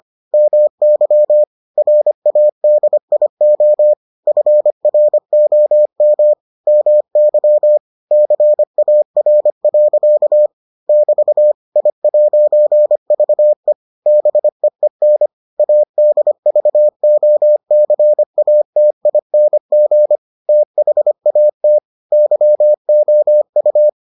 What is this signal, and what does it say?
Morse code, heard as a keyed tone, about 25 wpm, 600 Hz. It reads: MY RADIO FROM MY CAR. <BT> I'VE BEEN ADVOCATING THAT YOU